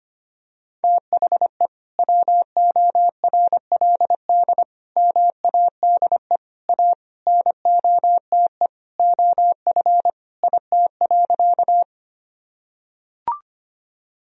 Morse code, heard as a keyed tone, about 25 wpm, 700 Hz. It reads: THE WORLD MADE A NOTE OF IT.
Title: Sentences from Top 1000 Words - Review Part 2 - 25wpm